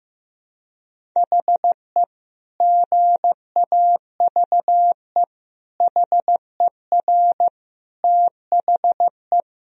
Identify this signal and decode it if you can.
Morse code, heard as a keyed tone, about 15 wpm, 700 Hz. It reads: HE GAVE HER THE
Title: Sentences from Top 1000 Words - Review Part 1 - 15wpm